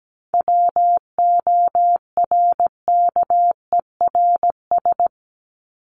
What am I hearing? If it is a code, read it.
Morse code, heard as a keyed tone, about 17 wpm, 700 Hz. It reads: WORKERS
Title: ES:S Suffix - Top 100 Words with Base Words 17wpm